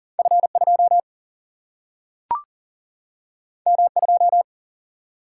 Morse code, heard as a keyed tone, about 40 wpm, 700 Hz. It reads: F2 M2
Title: Single-Letter Single-Number 40wpm